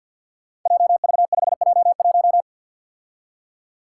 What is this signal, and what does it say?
Morse code, heard as a keyed tone, about 52 wpm, 700 Hz. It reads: JV5J1
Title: Five Challenging Letter-Number Combinations 50wpm